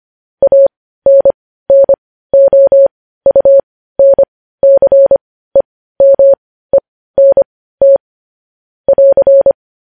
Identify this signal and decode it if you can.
Morse code, heard as a keyed tone, about 25 wpm, spaced as for 15 wpm, 550 Hz. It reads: ANNOUNCEMENT <AR>
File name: podcast / News headlines-1x-25x15